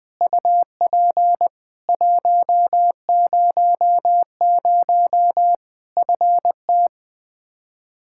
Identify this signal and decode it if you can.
Morse code, heard as a keyed tone, about 21 wpm, 700 Hz. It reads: UP 100 FT
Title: K1USN SST QSO Practice 20wpm